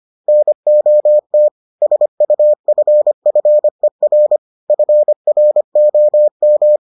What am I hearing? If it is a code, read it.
Morse code, heard as a keyed tone, about 25 wpm, 600 Hz. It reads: NOT SUFFER FROM